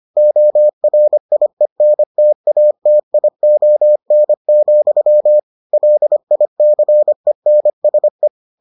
Morse code, heard as a keyed tone, about 25 wpm, 600 Hz. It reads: ORIENTATION, LICENSE